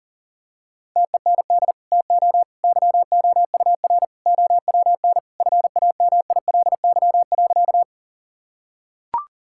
Morse code, heard as a keyed tone, about 40 wpm, 700 Hz. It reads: TEND TO YOUR OWN FAMILY.
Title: Sentences from Top 1000 Words - Review Part 5 - 40wpm